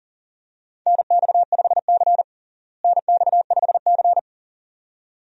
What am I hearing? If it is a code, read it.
Morse code, heard as a keyed tone, about 40 wpm, 700 Hz. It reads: NX5C NX5C
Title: USA Call Signs - 4 Characters - Repeated 2x - 40wpm